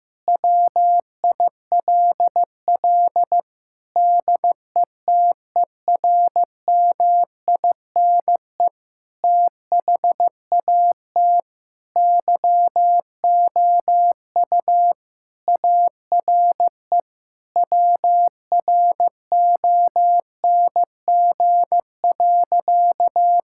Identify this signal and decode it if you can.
Morse code, heard as a keyed tone, about 15 wpm, 700 Hz. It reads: WILL DETERMINE THAT YOU ARE WRONG.
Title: Sentences from Top 1000 Words - Review Part 5 - 15wpm